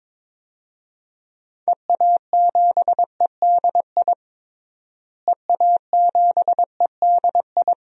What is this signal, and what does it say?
Morse code, heard as a keyed tone, about 21 wpm, 700 Hz. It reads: EA7EDI EA7EDI
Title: DXCC-100 Call Signs - 6 Characters - Repeated 2x - 22wpm